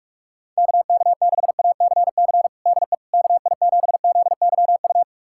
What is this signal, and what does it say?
Morse code, heard as a keyed tone, about 45 wpm, 700 Hz. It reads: KK6ACC DE KI7ZYU
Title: USA Callsign DE Callsign - 6 Characters - 45wpm